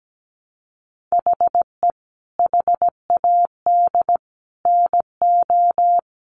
Morse code, heard as a keyed tone, about 17 wpm, 700 Hz. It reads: HE HAD NO
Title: Sentences from Top 1100 Words 17wpm